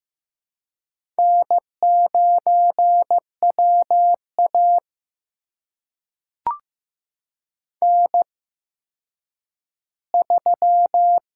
Morse code, heard as a keyed tone, about 15 wpm, spaced as for 9 wpm, 700 Hz. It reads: N9WA N3
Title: USA Call Signs - 4 Characters - 8x Character Spacing - 15wpm